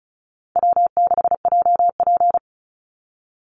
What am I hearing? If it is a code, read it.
Morse code, heard as a keyed tone, about 34 wpm, 700 Hz. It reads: W6JP
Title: NA SOTA Chaser Callsigns 35wpm